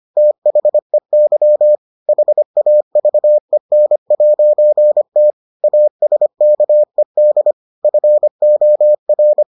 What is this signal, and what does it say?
Morse code, heard as a keyed tone, about 25 wpm, 600 Hz. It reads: THEY HAVEN'T ASKED FOR